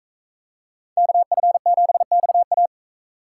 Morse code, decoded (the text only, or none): KF7XA